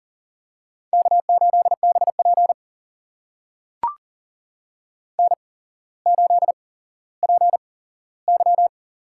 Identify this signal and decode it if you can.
Morse code, heard as a keyed tone, about 40 wpm, spaced as for 17 wpm, 700 Hz. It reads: K8BP N8PY
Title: USA Call Signs - 4 Characters - 8x Character Spacing - 40wpm